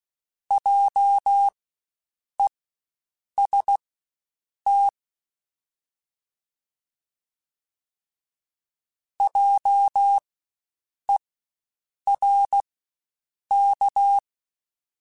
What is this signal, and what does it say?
Morse code, heard as a keyed tone, about 16 wpm, spaced as for 7 wpm, 800 Hz. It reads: JEST JERK
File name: Spaced Alphas / Spaced Alphas mp3s / Js sp @15